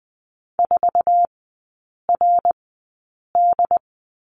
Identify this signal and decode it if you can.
Morse code, heard as a keyed tone, about 20 wpm, spaced as for 8 wpm, 690 Hz. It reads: 4RD
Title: 087 Lesson 15e - Number 5 - Three Characters 20wpm